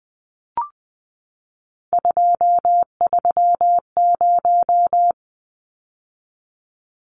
Morse code, heard as a keyed tone, about 20 wpm, 700 Hz. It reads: 230
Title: Numbers - 3-Digit 20wpm